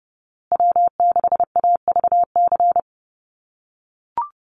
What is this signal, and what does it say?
Morse code, heard as a keyed tone, about 30 wpm, 700 Hz. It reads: W6AVC
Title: USA Call Signs Repeated 3x 30wpm